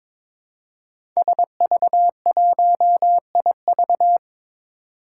Morse code, heard as a keyed tone, about 22 wpm, 700 Hz. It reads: SV1IV
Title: Five Challenging Letter-Number Combinations 22wpm